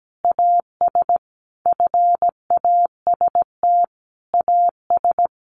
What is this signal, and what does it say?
Morse code, heard as a keyed tone, about 17 wpm, 700 Hz. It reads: AS FAST AS
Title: Sentences from Top 600 Words 17wpm